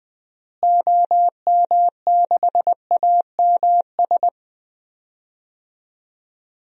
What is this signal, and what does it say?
Morse code, heard as a keyed tone, about 20 wpm, 700 Hz. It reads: OM6AMS
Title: DXCC-100 Call Signs - 6 Characters - Repeated 2x - 20wpm